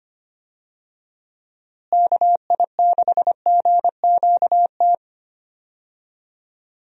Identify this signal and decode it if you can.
Morse code, heard as a keyed tone, about 25 wpm, 700 Hz. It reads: KI6GQT
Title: USA Call Signs - 6 Characters - Repeated 2x - 25wpm